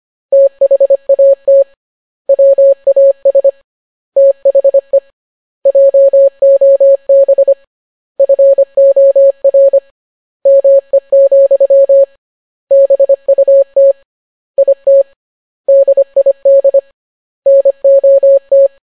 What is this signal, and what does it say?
Morse code, heard as a keyed tone, about 25 wpm, 550 Hz. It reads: THAT WAS THE JOB FOR ME, BUT IT DID NOT